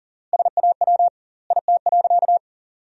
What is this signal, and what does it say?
Morse code, heard as a keyed tone, about 37 wpm, 700 Hz. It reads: SAW IT.
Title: 2x Word-Spacing - Sentences from Top 300 Words - 40wpm